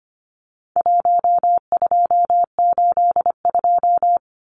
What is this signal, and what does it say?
Morse code, heard as a keyed tone, about 25 wpm, 700 Hz. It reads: 1282
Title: Numbers - 4-Digit 25wpm